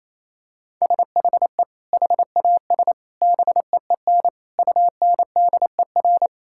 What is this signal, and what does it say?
Morse code, heard as a keyed tone, about 28 wpm, 700 Hz. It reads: SHE HAS BEEN UNDER
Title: Sentences from Top 1000 Words - Review Part 6 - 28wpm